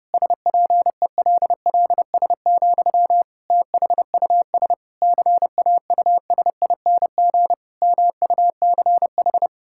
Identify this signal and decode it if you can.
Morse code, heard as a keyed tone, about 30 wpm, 700 Hz. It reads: SPELLS, THUS CAUSING MUCH
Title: Pangram Challenge v2 - 30wpm